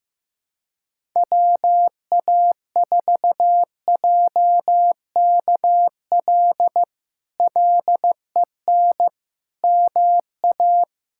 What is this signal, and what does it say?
Morse code, heard as a keyed tone, about 15 wpm, 700 Hz. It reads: WA4JKL LEN MA
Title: K1USN SST QSO Practice - 15wpm